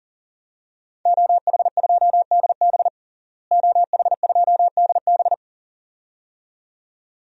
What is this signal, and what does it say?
Morse code, heard as a keyed tone, about 40 wpm, 700 Hz. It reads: OH2DB OH2DB
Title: DXCC-100 Call Signs - 5 Characters - Repeated 2x - 40wpm